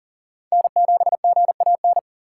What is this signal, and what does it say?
Morse code, heard as a keyed tone, about 40 wpm, 690 Hz. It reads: N7GAN
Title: USA Call Signs - 4x Character Spacing - 40wpm